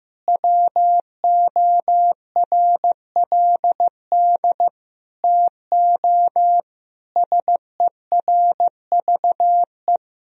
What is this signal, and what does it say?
Morse code, heard as a keyed tone, about 15 wpm, 700 Hz. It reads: WORLD TO SERVE
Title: Sentences from Top 1000 Words - Review Part 2 - 15wpm